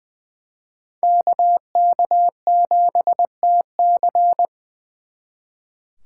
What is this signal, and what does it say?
Morse code, heard as a keyed tone, about 24 wpm, 700 Hz. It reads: KK7TC
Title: Speed Racing - USA 2x2 Call Signs 20wpm